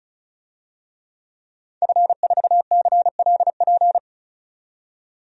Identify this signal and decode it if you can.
Morse code, heard as a keyed tone, about 34 wpm, 700 Hz. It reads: F4CLP
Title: DXCC-100 Call Signs - 5 Characters - 35wpm